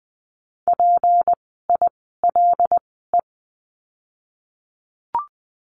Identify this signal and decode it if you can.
Morse code, heard as a keyed tone, about 19 wpm, spaced as for 14 wpm, 700 Hz. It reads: PILE
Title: 153 Lesson 25f - Letter p - Words 20wpm